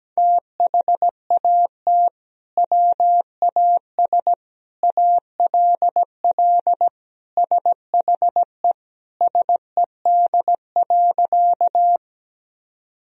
Morse code, heard as a keyed tone, about 17 wpm, 700 Hz. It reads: THAT WAS ALL SHE SED.
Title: Sentences from Top 100 Words and Abbreviations 17wpm